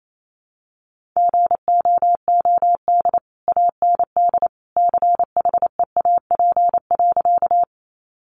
Morse code, heard as a keyed tone, about 28 wpm, 700 Hz. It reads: GOOD AND CHEAP.
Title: Sentences from Top 1200 Words 28wpm